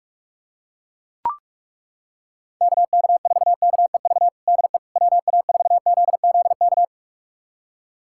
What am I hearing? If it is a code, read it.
Morse code, heard as a keyed tone, about 41 wpm, 700 Hz. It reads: KK4KEV DE WA4ZZK
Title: USA Callsign DE Callsign - 6 Characters - 45wpm